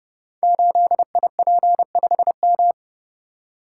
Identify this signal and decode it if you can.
Morse code, heard as a keyed tone, about 30 wpm, 700 Hz. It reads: 8IP5M
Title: Code Groups - 5 Letter:Numbers - 30wpm